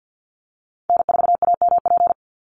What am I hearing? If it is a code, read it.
Morse code, heard as a keyed tone, about 50 wpm, 700 Hz. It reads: N4AMP
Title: USA Call Signs 50wpm